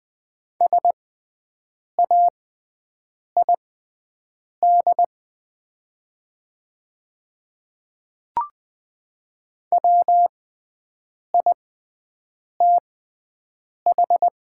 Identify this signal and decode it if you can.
Morse code, heard as a keyed tone, about 20 wpm, spaced as for 7 wpm, 700 Hz. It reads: SAID WITH
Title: 6x Character Spacing - Top 100 Words - 20wpm